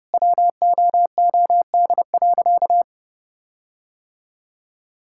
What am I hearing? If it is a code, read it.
Morse code, heard as a keyed tone, about 30 wpm, 700 Hz. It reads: WOOD.
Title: Sentences from Top 1600 Words 30wpm